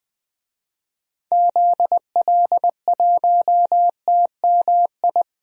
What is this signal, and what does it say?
Morse code, heard as a keyed tone, about 21 wpm, 700 Hz. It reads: ZL1TMI
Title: DXCC-100 Call Signs - 6 Characters - 20wpm